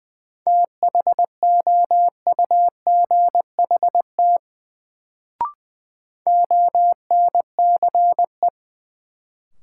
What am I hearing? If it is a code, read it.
Morse code, heard as a keyed tone, about 20 wpm, 700 Hz. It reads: THOUGHT ONCE
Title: Sentences from Top 300 Words - New Word Drill - 20wpm